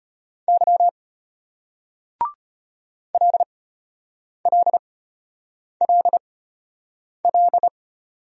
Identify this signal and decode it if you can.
Morse code, heard as a keyed tone, about 35 wpm, spaced as for 8 wpm, 700 Hz. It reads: Y LLLL